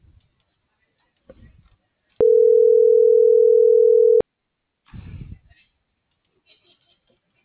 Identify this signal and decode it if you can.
Ringback tone (North America), 440 + 480 Hz.